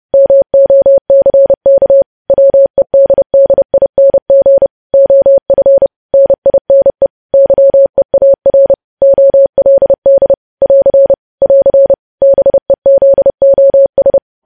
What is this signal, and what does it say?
Morse code, heard as a keyed tone, about 30 wpm, 550 Hz. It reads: MOCK WEDDING OF NINE YEAR OLD <AR> <AR> BEZOS
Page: News Headlines in Morse Code at 30 WPM